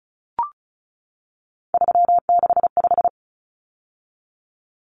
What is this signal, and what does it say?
Morse code, heard as a keyed tone, about 34 wpm, 700 Hz. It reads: 365
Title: Common Cut Numbers - 3-Digit 35wpm